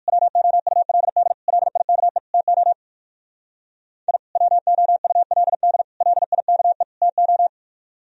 Morse code, heard as a keyed tone, about 58 wpm, 700 Hz. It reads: WOULD LIKE TO I WOULD LIKE TO